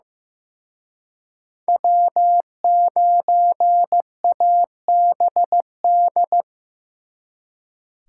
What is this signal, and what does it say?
Morse code, heard as a keyed tone, about 18 wpm, 700 Hz. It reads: W9ABD
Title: Speed Racing - USA 1x3 Call Signs 15wpm